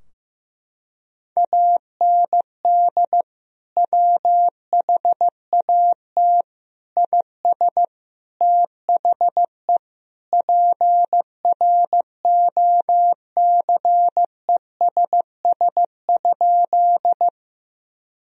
Morse code, heard as a keyed tone, about 15 wpm, 700 Hz. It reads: AND WHAT IS THE PROCESS?
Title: Sentences from Top 1000 Words - Review Part 3 - 15wpm